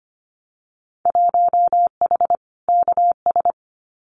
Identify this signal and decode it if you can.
Morse code, heard as a keyed tone, about 25 wpm, 700 Hz. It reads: 1H KS
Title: Winter Field Day Exchanges 25wpm